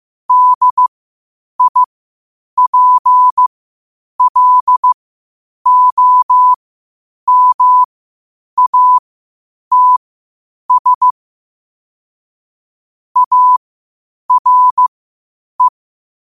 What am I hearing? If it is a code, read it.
Morse code, heard as a keyed tone, about 15 wpm, spaced as for 8 wpm, 1000 Hz. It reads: DIPLOMATS ARE
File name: qotdfiles / 8wpm